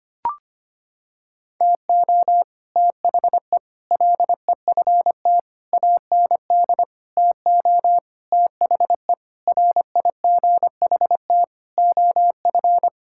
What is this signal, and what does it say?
Morse code, heard as a keyed tone, about 25 wpm, 700 Hz. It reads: TO THE LEFT AND TO THE RIGHT OF
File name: Sets of 9 Words - 25wpm